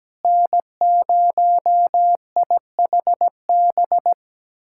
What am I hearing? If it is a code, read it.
Morse code, heard as a keyed tone, about 17 wpm, 700 Hz. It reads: N0IHB
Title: USA Callsign DE Callsign - 5 Characters - 17wpm